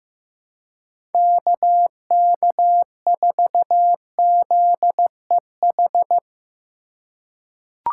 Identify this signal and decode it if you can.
Morse code, heard as a keyed tone, about 15 wpm, 700 Hz. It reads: KK4ZEH
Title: USA Call Signs - 6 Characters - 3x Character Spacing - 15wpm